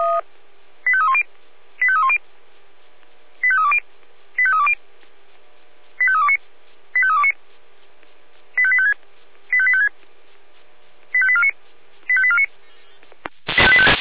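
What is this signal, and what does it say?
Five-tone selective call (ZVEI), heard as quick runs of five tones, tones of about 70 ms. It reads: 85310 85310 85320 85320 85329 85329 85856 85856 85849 85849
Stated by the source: ZVEI five-tone selective call received by SDR.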